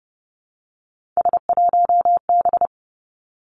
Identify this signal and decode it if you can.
Morse code, heard as a keyed tone, about 30 wpm, 700 Hz. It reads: S1B
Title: Three Challenging Letter-Number Combinations 30wpm